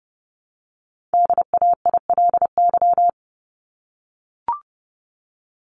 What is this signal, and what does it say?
Morse code, heard as a keyed tone, about 30 wpm, 700 Hz. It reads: DAILY
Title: LY Suffix - Top 100 Words with Base Words 30wpm